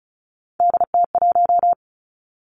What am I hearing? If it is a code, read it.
Morse code, heard as a keyed tone, about 36 wpm, 700 Hz. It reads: DT1
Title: Three Challenging Letter-Number Combinations 35wpm